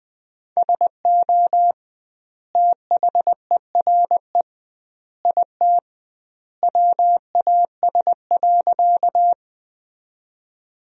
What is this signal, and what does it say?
Morse code, heard as a keyed tone, about 20 wpm, 700 Hz. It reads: SO THERE IT WAS.
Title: 2x Word-Spacing - Sentences from Top 100 Words - 20wpm